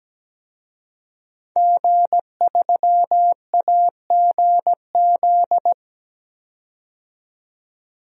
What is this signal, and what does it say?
Morse code, heard as a keyed tone, about 17 wpm, 700 Hz. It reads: G3AGZ
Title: DXCC-100 Call Signs - 5 Characters - 17wpm